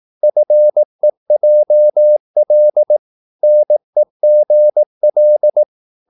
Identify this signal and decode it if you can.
Morse code, heard as a keyed tone, about 18 wpm, 600 Hz. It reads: FEJL NEGL
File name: Korte ord DK 18wpm